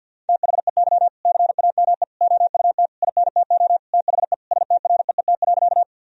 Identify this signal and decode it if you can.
Morse code, heard as a keyed tone, about 50 wpm, 700 Hz. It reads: THEY CAME OUT INTO THE STREET.
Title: Sentences from Top 1000 Words - Review Part 3 - 50wpm